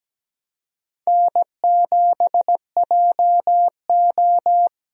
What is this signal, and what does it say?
Morse code, heard as a keyed tone, about 17 wpm, 700 Hz. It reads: N7JO
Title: USA Call Signs - 4 Characters - 17wpm